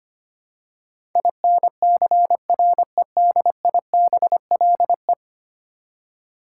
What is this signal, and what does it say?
Morse code, heard as a keyed tone, about 25 wpm, 700 Hz. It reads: INCREDIBLE
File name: Ten Letter Words - 25wpm